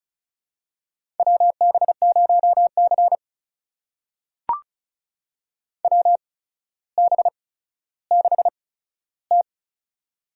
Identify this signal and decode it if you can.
Morse code, heard as a keyed tone, about 36 wpm, spaced as for 15 wpm, 700 Hz. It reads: WB0C WB6T
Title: USA Call Signs - 4 Characters - 8x Character Spacing - 35wpm